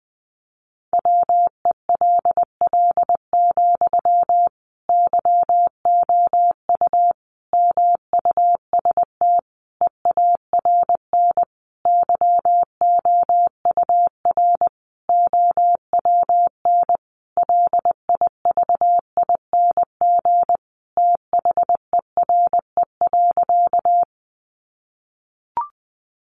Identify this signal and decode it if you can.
Morse code, heard as a keyed tone, about 19 wpm, 700 Hz. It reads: WELL, YOU MUST EARN YOUR OWN LIVING THERE.
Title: Sentences from Top 1500 Words 20wpm